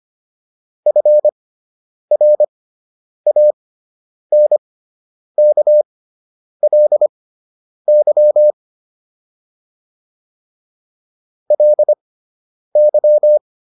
Morse code, heard as a keyed tone, about 25 wpm, spaced as for 9 wpm, 600 Hz. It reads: FRANKLY LY
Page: fundamental-practice-files